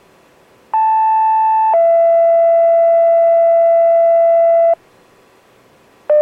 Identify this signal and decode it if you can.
Two-tone page, heard as a one-second tone, then a three-second tone, about 880 then 670 Hz. It is heard from 0.7 to 4.7 s.